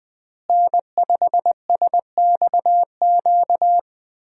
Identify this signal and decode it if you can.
Morse code, heard as a keyed tone, about 21 wpm, 700 Hz. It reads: N5SXQ
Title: USA Call Signs - 5 Characters - Repeated 2x - 20wpm